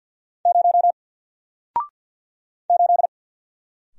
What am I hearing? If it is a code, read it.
Morse code, heard as a keyed tone, about 48 wpm, 700 Hz. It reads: 0 8
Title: Single Character (Letter:Number:Punctuation:Prosign) Alt 50wpm